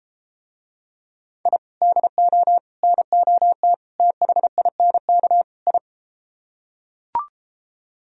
Morse code, heard as a keyed tone, about 33 wpm, 700 Hz. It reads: I DO NOT THINK I